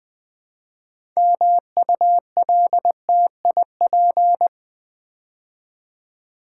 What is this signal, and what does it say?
Morse code, heard as a keyed tone, about 20 wpm, 700 Hz. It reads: MULTIP
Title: Build A Word - 2 Letters At A Time - 8-Letter Words - 20wpm